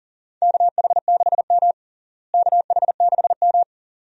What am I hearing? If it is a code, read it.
Morse code, heard as a keyed tone, about 40 wpm, 700 Hz. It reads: KH6M KH6M
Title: DXCC-100 Call Signs - 4 Characters - Repeated 2x - 40wpm